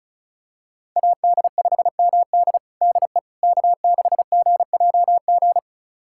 Morse code, heard as a keyed tone, about 34 wpm, 700 Hz. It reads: AD5MD DE K6GJG
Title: USA Callsign DE Callsign - 5 Characters - 35wpm